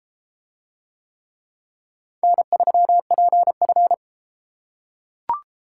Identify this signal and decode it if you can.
Morse code, heard as a keyed tone, about 33 wpm, 700 Hz. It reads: N3PF